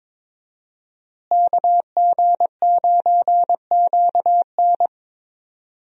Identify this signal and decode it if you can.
Morse code, heard as a keyed tone, about 22 wpm, 700 Hz. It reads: KG9QN
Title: USA Call Signs - 5 Characters - Repeated 2x - 22wpm